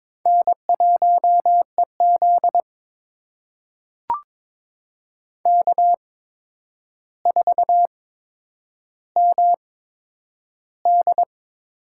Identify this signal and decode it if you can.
Morse code, heard as a keyed tone, about 22 wpm, spaced as for 9 wpm, 700 Hz. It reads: N1EZ K4MD
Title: USA Call Signs - 4 Characters - 8x Character Spacing - 22wpm